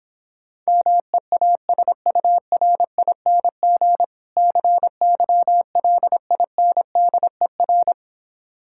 Morse code, heard as a keyed tone, about 26 wpm, 700 Hz. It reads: MEASURING CYLINDER